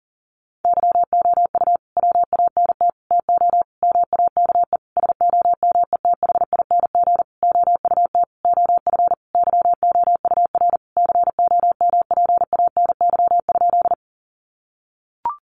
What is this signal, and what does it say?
Morse code, heard as a keyed tone, about 37 wpm, 700 Hz. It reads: YOU WANT TO MAKE SOMETHING OUT OF YOUR COMPANY?
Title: Sentences from Top 600 Words 40wpm